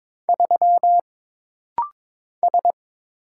Morse code, heard as a keyed tone, about 22 wpm, 700 Hz. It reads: 3 S
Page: This practice set is composed of Letters, Numbers, common punctuations, and common prosigns.